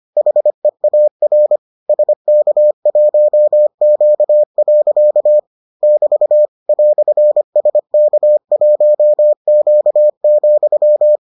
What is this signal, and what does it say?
Morse code, heard as a keyed tone, about 25 wpm, 600 Hz. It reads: HEAR SK1Q. <BT> "SK1Q,